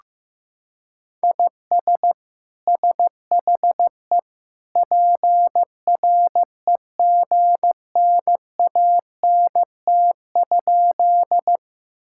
Morse code, heard as a keyed tone, about 15 wpm, 700 Hz. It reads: IS SHE PREGNANT?
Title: Sentences from Top 1000 Words - Review Part 6 - 15wpm